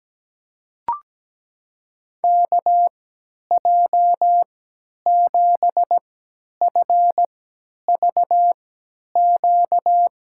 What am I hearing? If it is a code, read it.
Morse code, heard as a keyed tone, about 17 wpm, spaced as for 10 wpm, 700 Hz. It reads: KJ7FVQ
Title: USA Call Signs - 3x Character Spacing - 17wpm